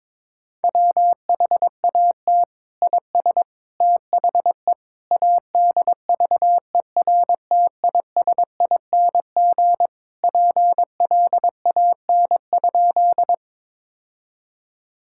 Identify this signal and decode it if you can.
Morse code, heard as a keyed tone, about 22 wpm, 700 Hz. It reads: WHAT IS THE ADVERTISING PLAN?
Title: Sentences from Top 1000 Words - Review Part 6 - 22wpm